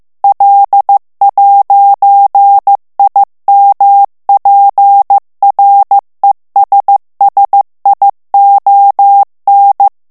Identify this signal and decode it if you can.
Morse code, heard as a keyed tone, about 15 wpm, 800 Hz. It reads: L'IMPRESSION
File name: morse mp3 / FRANCAIS TEXTE CLAIR 1 VITESSE 14 MOTS MINUTES